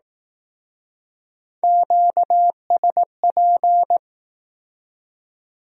Morse code, heard as a keyed tone, about 19 wpm, 700 Hz. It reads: QSP